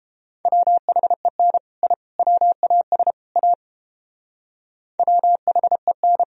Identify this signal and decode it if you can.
Morse code, heard as a keyed tone, about 29 wpm, 700 Hz. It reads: WHEN I WAS A WHEN